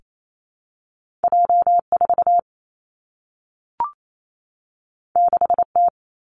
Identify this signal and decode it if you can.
Morse code, heard as a keyed tone, about 28 wpm, 700 Hz. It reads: J 4 6 T
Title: Two Challenging Letter-Number Combinations 28wpm